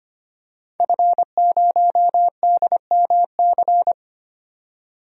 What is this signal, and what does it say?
Morse code, heard as a keyed tone, about 25 wpm, 700 Hz. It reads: F0DMC
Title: DXCC-100 Call Signs - 5 Characters - 25wpm